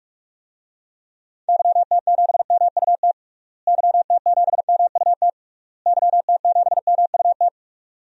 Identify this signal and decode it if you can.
Morse code, heard as a keyed tone, about 46 wpm, 700 Hz. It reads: YT7MUT YT7MUT YT7MUT
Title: DXCC-100 Call Signs - 6 Characters - Repeated 3x - 45wpm